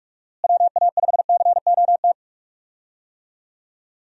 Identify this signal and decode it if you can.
Morse code, heard as a keyed tone, about 46 wpm, 700 Hz. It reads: WA5COT
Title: USA Call Signs - 6 Characters - Repeated 3x - 45wpm